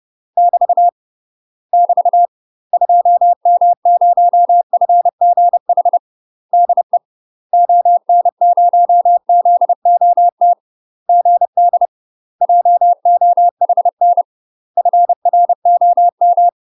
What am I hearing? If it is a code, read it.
Morse code, heard as a keyed tone, about 30 wpm, 700 Hz. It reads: <BT> <BT> 2M0FGH DE ON0ZOT GD JOHN FROM
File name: UK MASTER QSOs TRAINING / 30WPM / UK MASTER QSOs 0044